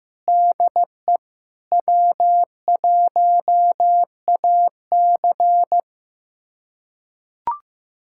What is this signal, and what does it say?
Morse code, heard as a keyed tone, about 15 wpm, 700 Hz. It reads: DE W1AC
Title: USA Callsign DE Callsign - 4 Characters - 15wpm